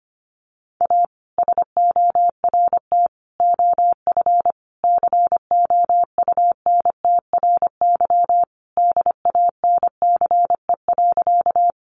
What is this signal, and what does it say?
Morse code, heard as a keyed tone, about 25 wpm, 700 Hz. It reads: A SORT OF COUNTRY DANCE.
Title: Sentences from Top 1300 Words 25wpm